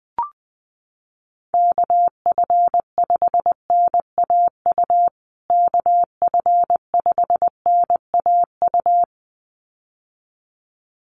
Morse code, heard as a keyed tone, about 20 wpm, 700 Hz. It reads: KF5NAU KF5NAU
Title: USA Call Signs Repeated 2x 20wpm